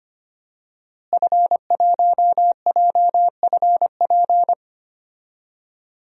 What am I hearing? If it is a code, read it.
Morse code, heard as a keyed tone, about 25 wpm, 700 Hz. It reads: F1JFP
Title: DXCC-100 Call Signs - 5 Characters - Repeated 2x - 25wpm